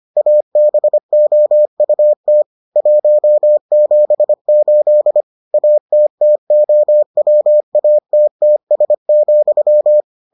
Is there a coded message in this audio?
Yes: Morse code from 0.2 to 10.0 s, heard as a keyed tone, about 25 wpm, 600 Hz. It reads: ABOUT 178 ATTOWATTS,